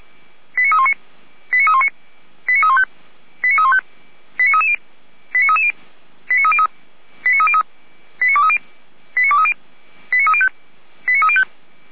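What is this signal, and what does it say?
Five-tone selective call (ZVEI), heard as quick runs of five tones, tones of about 70 ms. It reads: 89319 89319 89315 89315 89330 89330 89303 89303 89230 89230 89305 89305
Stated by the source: ZVEI five-tone selective call 89125 received by SDR.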